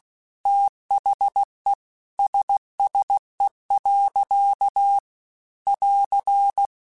Morse code, heard as a keyed tone, about 16 wpm, 800 Hz. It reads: THE SSE. <AR>
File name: Weather Sets / Weather Sets-mp3s / Weather 4 14 @15